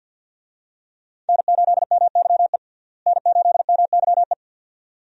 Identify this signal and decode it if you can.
Morse code, heard as a keyed tone, about 47 wpm, 700 Hz. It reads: N8MYE N8MYE
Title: USA Call Signs - 5 Characters - Repeated 2x - 50wpm